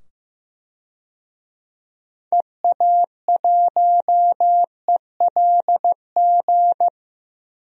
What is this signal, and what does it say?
Morse code, heard as a keyed tone, about 15 wpm, 700 Hz. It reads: EA1ELG
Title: DXCC-100 Call Signs - 6 Characters - 15wpm